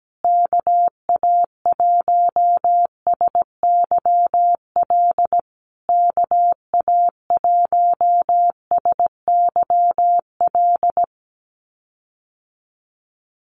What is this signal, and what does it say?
Morse code, heard as a keyed tone, about 17 wpm, 700 Hz. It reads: KA1SYL KA1SYL
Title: USA Call Signs Repeated 3x 17wpm